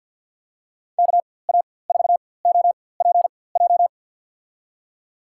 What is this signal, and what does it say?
Morse code, heard as a keyed tone, about 52 wpm, spaced as for 24 wpm, 700 Hz. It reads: KA4OPJ
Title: USA Call Signs - 6 Characters - 4x Character Spacing - 50wpm